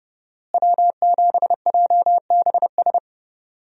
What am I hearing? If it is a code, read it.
Morse code, heard as a keyed tone, about 30 wpm, 700 Hz. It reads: W7JBS
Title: USA Call Signs - 5 Characters - Repeated 2x - 30wpm